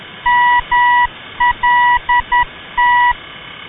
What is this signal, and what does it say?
Morse code, heard as a keyed tone, about 10 wpm, 970 Hz. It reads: MLT
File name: ndb malta mlt-395